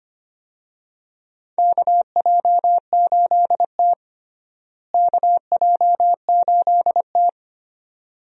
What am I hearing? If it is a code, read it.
Morse code, heard as a keyed tone, about 25 wpm, 700 Hz. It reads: KJ8T KJ8T
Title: USA Call Signs - 4 Characters - Repeated 2x - 25wpm